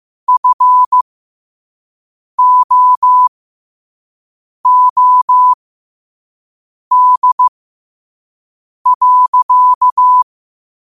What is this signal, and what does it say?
Morse code, heard as a keyed tone, about 15 wpm, spaced as for 5 wpm, 1000 Hz. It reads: FOOD.
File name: qotdfiles / 5wpm